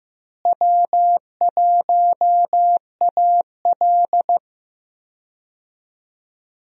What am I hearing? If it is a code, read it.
Morse code, heard as a keyed tone, about 15 wpm, 700 Hz. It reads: W1AL
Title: USA Call Signs - 4 Characters - Repeated 2x - 15wpm